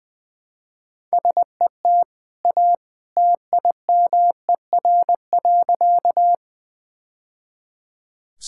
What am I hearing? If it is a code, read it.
Morse code, heard as a keyed tone, about 21 wpm, 700 Hz. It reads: SET A TIMER.
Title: Easy Sentences with Hints - 20wpm